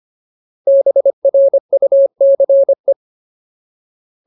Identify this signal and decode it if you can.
Morse code, heard as a keyed tone, about 25 wpm, 550 Hz. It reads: BRUCE
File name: Ultimate Word List / 25WPM Ultimate